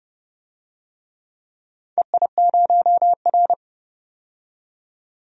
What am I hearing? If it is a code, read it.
Morse code, heard as a keyed tone, about 30 wpm, 700 Hz. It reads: EI0R
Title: DXCC-100 Call Signs - 4 Characters - 30wpm